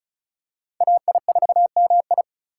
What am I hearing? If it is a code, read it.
Morse code, heard as a keyed tone, about 36 wpm, 700 Hz. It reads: AI4MI
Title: K1USN SST QSO Practice - 35wpm